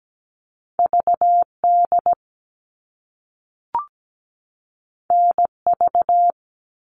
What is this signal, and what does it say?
Morse code, heard as a keyed tone, about 17 wpm, 700 Hz. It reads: VD NV
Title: All Cut Numbers - 2-Digit 17wpm